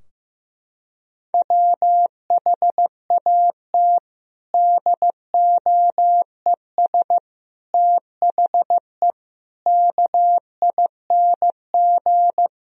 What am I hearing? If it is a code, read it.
Morse code, heard as a keyed tone, about 15 wpm, 700 Hz. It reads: WHAT DOES THE KING
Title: Sentences from Top 1000 Words - Review Part 2 - 15wpm